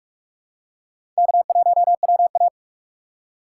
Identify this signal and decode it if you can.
Morse code, heard as a keyed tone, about 45 wpm, 700 Hz. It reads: K1WA
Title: USA Call Signs - 4 Characters - 4x Character Spacing - 45wpm